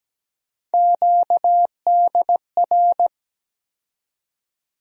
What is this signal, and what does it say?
Morse code, heard as a keyed tone, about 17 wpm, 700 Hz. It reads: QDR
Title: USA Call Sign Postfixes - 3 Letters 17wpm